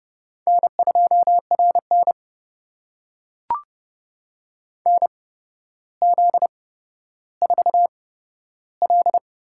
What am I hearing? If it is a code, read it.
Morse code, heard as a keyed tone, about 30 wpm, spaced as for 13 wpm, 700 Hz. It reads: N2RN NZ4L
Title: USA Call Signs - 8x Character Spacing - 30wpm